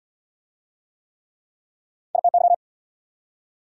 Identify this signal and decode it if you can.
Morse code, heard as a keyed tone, about 69 wpm, 700 Hz. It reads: ITY